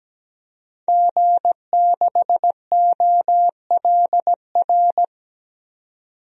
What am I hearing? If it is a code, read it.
Morse code, heard as a keyed tone, about 17 wpm, 700 Hz. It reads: G6OLR
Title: DXCC-100 Call Signs - 5 Characters - 17wpm